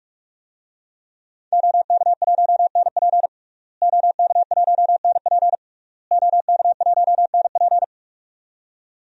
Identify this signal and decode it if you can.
Morse code, heard as a keyed tone, about 41 wpm, 700 Hz. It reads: OK1NP OK1NP OK1NP
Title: DXCC-100 Call Signs - 5 Characters - Repeated 3x - 45wpm